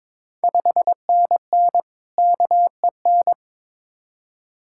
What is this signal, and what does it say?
Morse code, heard as a keyed tone, about 22 wpm, 700 Hz. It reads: 5NN KEN
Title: Ontario QSO Party Exchange 22wpm